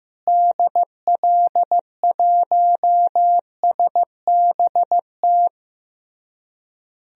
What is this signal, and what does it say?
Morse code, heard as a keyed tone, about 15 wpm, 700 Hz. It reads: DL1SBT
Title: DXCC-100 Call Signs - 6 Characters - Repeated 3x - 15wpm